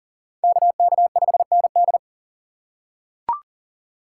Morse code, heard as a keyed tone, about 39 wpm, 700 Hz. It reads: KK5ND
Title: USA Call Signs - 3x Character Spacing - 40wpm